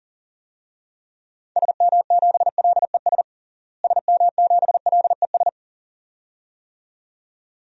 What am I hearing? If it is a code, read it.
Morse code, heard as a keyed tone, about 37 wpm, 700 Hz. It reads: SM7LES SM7LES
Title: DXCC-100 Call Signs - 6 Characters - Repeated 2x - 40wpm